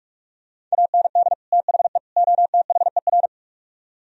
Morse code, heard as a keyed tone, about 45 wpm, 700 Hz. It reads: AND THE OTHER